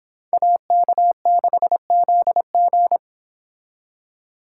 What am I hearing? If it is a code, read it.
Morse code, heard as a keyed tone, about 29 wpm, 700 Hz. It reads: AK6ZG